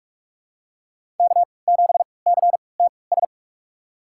Morse code, heard as a keyed tone, about 47 wpm, spaced as for 26 wpm, 700 Hz. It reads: K7CTS
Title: USA Call Signs - 3x Character Spacing - 45wpm